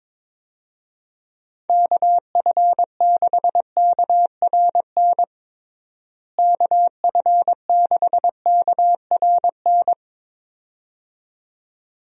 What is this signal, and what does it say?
Morse code, heard as a keyed tone, about 21 wpm, 700 Hz. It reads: KF6KRN KF6KRN
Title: USA Call Signs - 6 Characters - Repeated 2x - 22wpm